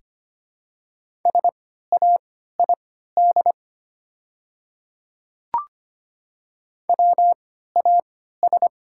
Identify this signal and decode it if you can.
Morse code, heard as a keyed tone, about 25 wpm, spaced as for 14 wpm, 700 Hz. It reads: SAID WAS
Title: 3x Character Spacing - Top 100 Words - 25wpm